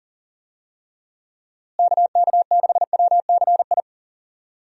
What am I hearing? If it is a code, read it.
Morse code, heard as a keyed tone, about 40 wpm, 700 Hz. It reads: KK6WCI
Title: USA Call Signs - 6 Characters - Repeated 2x - 40wpm